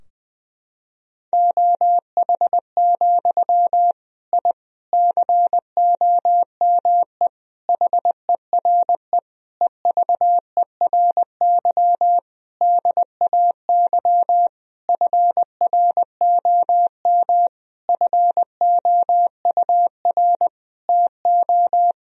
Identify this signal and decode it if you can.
Morse code, heard as a keyed tone, about 21 wpm, 700 Hz. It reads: OH, I COME HERE EVERY DAY FROM FOUR TO
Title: Sentences from Top 1000 Words - Review Part 2 - 20wpm